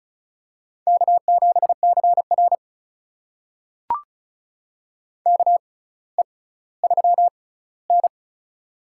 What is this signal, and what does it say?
Morse code, heard as a keyed tone, about 34 wpm, spaced as for 18 wpm, 700 Hz. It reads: K7CR KE3N
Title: USA Call Signs - 4 Characters - 6x Character Spacing - 35wpm